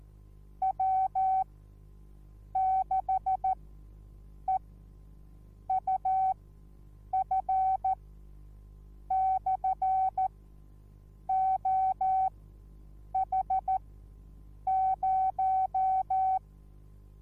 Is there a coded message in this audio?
Yes: Morse code from 0.6 to 16.4 s, heard as a keyed tone, about 14 wpm, spaced as for 7 wpm, 760 Hz. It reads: W6EUF/OH0